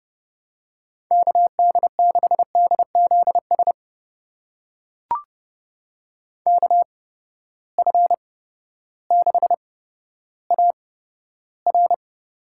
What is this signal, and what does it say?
Morse code, heard as a keyed tone, about 30 wpm, spaced as for 14 wpm, 700 Hz. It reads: KD6DZS KF6AR
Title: USA Call Signs - 6 Characters - 8x Character Spacing - 30wpm